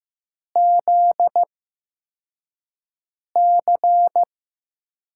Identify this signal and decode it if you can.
Morse code, heard as a keyed tone, about 16 wpm, spaced as for 4 wpm, 700 Hz. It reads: ZC
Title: USA Call Signs - 6 Characters - 8x Character Spacing - 15wpm